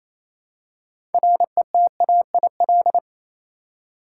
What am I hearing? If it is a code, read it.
Morse code, heard as a keyed tone, about 28 wpm, 700 Hz. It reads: RETAIL
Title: Practice QSO - Exchange 3 - More Info - 28wpm